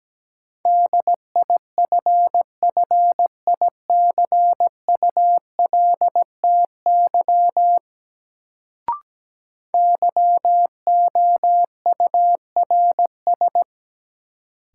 Morse code, heard as a keyed tone, about 17 wpm, 700 Hz. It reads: DIFFICULTY YOURS
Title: Sentences from Top 1500 Words - New Word Drill Alt - 17wpm